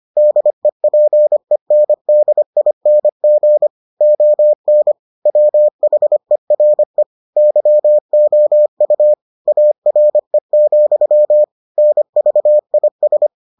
Morse code, heard as a keyed tone, about 25 wpm, 600 Hz. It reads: DEPENDING ON WHERE YOU ARE, NVIS